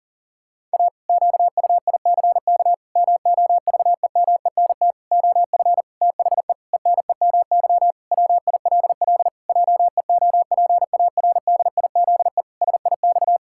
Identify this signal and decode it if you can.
Morse code, heard as a keyed tone, about 40 wpm, 700 Hz. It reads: A QUICK MOVEMENT OF THE ENEMY WILL JEOPARDIZE SIX
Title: Pangram Challenge v2 - 40wpm